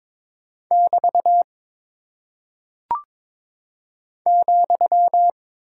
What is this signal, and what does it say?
Morse code, heard as a keyed tone, about 22 wpm, 700 Hz. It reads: <BT> ,
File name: practice / 22wpm / Single Punctuation-Prosign - 22wpm